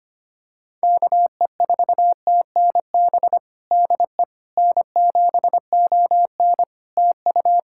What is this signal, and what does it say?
Morse code, heard as a keyed tone, about 25 wpm, 700 Hz. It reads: KE4TNB DE N7ON TU
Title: Practice QSO - Exchange 4 - Ending - 25wpm